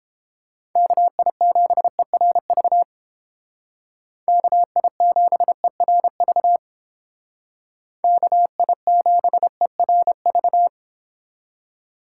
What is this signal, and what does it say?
Morse code, heard as a keyed tone, about 31 wpm, 700 Hz. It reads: KI7ERV KI7ERV KI7ERV